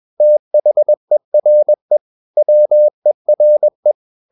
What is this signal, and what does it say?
Morse code, heard as a keyed tone, about 21 wpm, 600 Hz. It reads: THERE WERE
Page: CW med Gnister